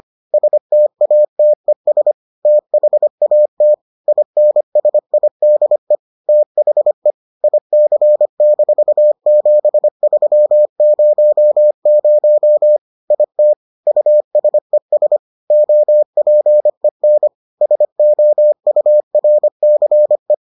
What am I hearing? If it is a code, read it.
Morse code, heard as a keyed tone, about 25 wpm, 600 Hz. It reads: STATES THAT INSIDE THE IC-7300 IT USES OPEN SOURCE